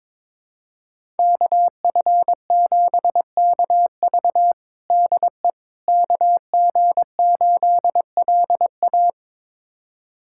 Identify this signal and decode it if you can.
Morse code, heard as a keyed tone, about 21 wpm, 700 Hz. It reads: KF7KV DE KG8LA
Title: USA Callsign DE Callsign - 5 Characters - 22wpm